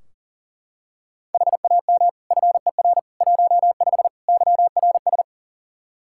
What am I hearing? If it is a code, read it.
Morse code, heard as a keyed tone, about 40 wpm, 700 Hz. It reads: HAM FER 15 YRS
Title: K1USN SST QSO Practice 40wpm